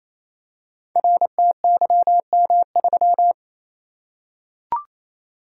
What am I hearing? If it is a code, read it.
Morse code, heard as a keyed tone, about 29 wpm, 700 Hz. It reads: RTYM3
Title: Code Groups - 5 Letter:Numbers - 28wpm